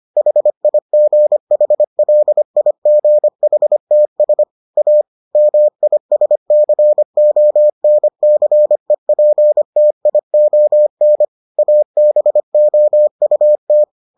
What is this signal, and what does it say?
Morse code, heard as a keyed tone, about 25 wpm, 600 Hz. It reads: HIGHLIGHTS A MISCONCEPTION ABOUT